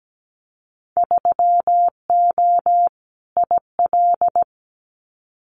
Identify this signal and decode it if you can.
Morse code, heard as a keyed tone, about 17 wpm, 700 Hz. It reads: 3O IL
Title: Winter Field Day Exchanges Encore 17wpm